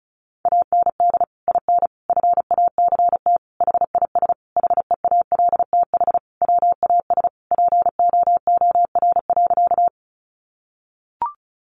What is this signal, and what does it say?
Morse code, heard as a keyed tone, about 34 wpm, 700 Hz. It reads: AND IN FACT HIS HEALTH WAS POOR.
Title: Sentences from Top 1000 Words 35wpm